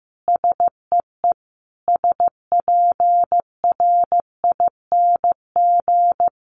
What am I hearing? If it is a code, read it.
Morse code, heard as a keyed tone, about 15 wpm, 700 Hz. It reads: SEE SPRING
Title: Sentences from Top 1200 Words 15wpm